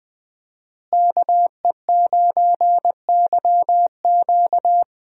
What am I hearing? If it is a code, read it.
Morse code, heard as a keyed tone, about 20 wpm, 700 Hz. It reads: KE9YQ
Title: USA Call Signs - 5 Characters - Repeated 2x - 20wpm